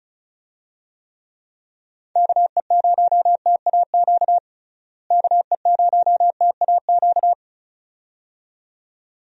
Morse code, heard as a keyed tone, about 36 wpm, 700 Hz. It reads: KE0TAQ KE0TAQ
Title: USA Call Signs - 6 Characters - Repeated 2x - 35wpm